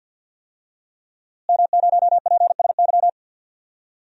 Morse code, heard as a keyed tone, about 51 wpm, 700 Hz. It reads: M0PSY
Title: DXCC-100 Call Signs - 5 Characters - Repeated 2x - 50wpm